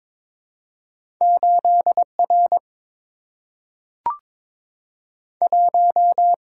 Morse code, heard as a keyed tone, about 22 wpm, 700 Hz. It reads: 8 R 1
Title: Single-Letter Single-Number 22wpm